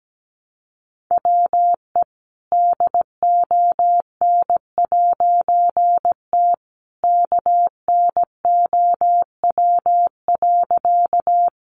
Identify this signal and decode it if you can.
Morse code, heard as a keyed tone, about 17 wpm, 700 Hz. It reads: WE DON'T KNOW.
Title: A Sound Of Thunder 17wpm